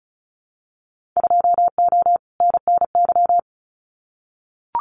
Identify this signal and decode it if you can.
Morse code, heard as a keyed tone, about 34 wpm, 700 Hz. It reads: 2O NNY
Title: Winter Field Day Exchanges 35wpm